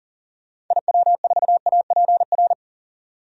Morse code, heard as a keyed tone, about 40 wpm, 700 Hz. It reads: IW4APR
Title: DXCC-100 Call Signs - 6 Characters - 40wpm